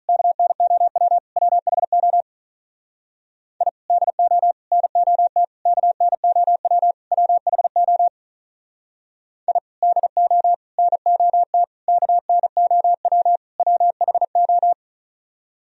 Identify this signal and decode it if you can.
Morse code, heard as a keyed tone, about 44 wpm, 700 Hz. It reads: KNOW WHO I DO NOT KNOW WHO I DO NOT KNOW WHO